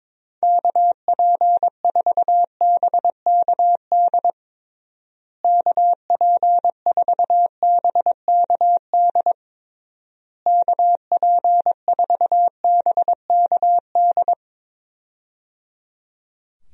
Morse code, heard as a keyed tone, about 22 wpm, 700 Hz. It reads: KP4BKD KP4BKD KP4BKD
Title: DXCC-100 Call Signs - 6 Characters - Repeated 3x - 22wpm